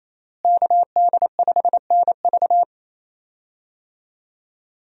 Morse code, heard as a keyed tone, about 27 wpm, 700 Hz. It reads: KD5NV
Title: USA Call Signs - 5 Characters - Repeated 3x - 28wpm